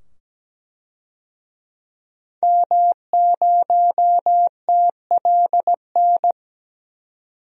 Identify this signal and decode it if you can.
Morse code, heard as a keyed tone, about 17 wpm, 700 Hz. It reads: M0TLN
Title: DXCC-100 Call Signs - 5 Characters - 17wpm